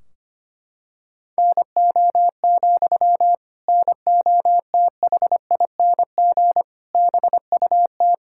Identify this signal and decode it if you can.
Morse code, heard as a keyed tone, about 25 wpm, 700 Hz. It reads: NO, NOTHING BUT
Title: Sentences from Top 1600 Words 25wpm